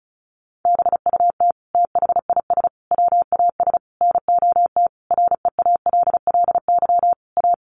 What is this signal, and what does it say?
Morse code, heard as a keyed tone, about 34 wpm, 700 Hz. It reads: BUT THIS WAS NOT REALLY A
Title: Sentences from Top 1500 Words 35wpm